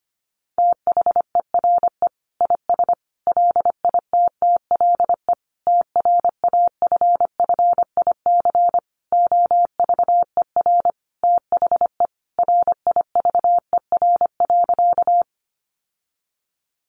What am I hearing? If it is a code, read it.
Morse code, heard as a keyed tone, about 25 wpm, 700 Hz. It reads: THERE IS LITTLE TRAFFIC OVER THE RIVER.
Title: Sentences from Top 1300 Words 25wpm